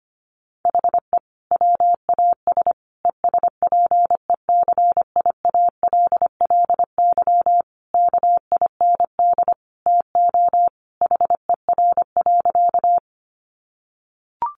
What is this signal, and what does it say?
Morse code, heard as a keyed tone, about 25 wpm, 700 Hz. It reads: HE WAS ESPECIALLY KIND TO HER.
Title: Sentences from Top 700 Words 25wpm